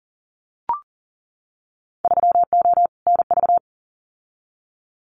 Morse code, heard as a keyed tone, about 40 wpm, 700 Hz. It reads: 3O NV
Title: Winter Field Day Exchanges Encore 40wpm